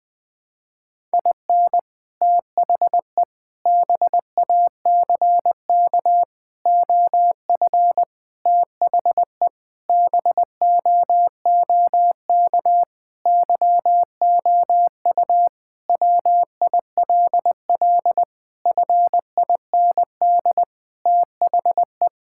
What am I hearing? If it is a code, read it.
Morse code, heard as a keyed tone, about 20 wpm, 700 Hz. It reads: IN THE BACK OF THE BOOK YOU WILL FIND THE
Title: Sentences with 5-Word Phrases - 20wpm